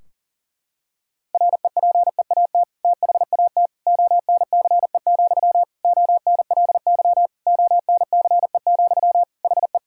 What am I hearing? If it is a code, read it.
Morse code, heard as a keyed tone, about 40 wpm, 700 Hz. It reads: REPEAT THAT ONCE, ONLY ONCE, HE
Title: Sentences from Top 1600 Words 40wpm